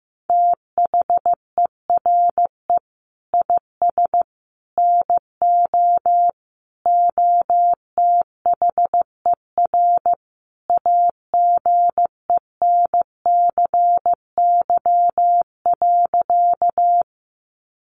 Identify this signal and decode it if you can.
Morse code, heard as a keyed tone, about 14 wpm, 700 Hz. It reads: THERE IS NO OTHER AGENCY.
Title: Sentences from Top 1300 Words 15wpm